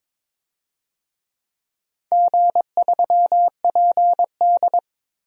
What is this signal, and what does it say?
Morse code, heard as a keyed tone, about 22 wpm, 700 Hz. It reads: G3PD
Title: DXCC-100 Call Signs - 4 Characters - Repeated 2x - 22wpm